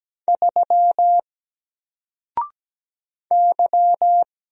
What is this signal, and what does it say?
Morse code, heard as a keyed tone, about 17 wpm, 700 Hz. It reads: 3 Y
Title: Single Character (Letter:Number:Punctuation:Prosign) 17wpm